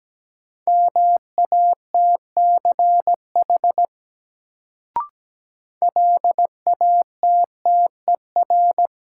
Morse code, heard as a keyed tone, about 17 wpm, 700 Hz. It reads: MATCH LATTER
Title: Sentences from Top 1600 Words - New Word Drill Alt - 17wpm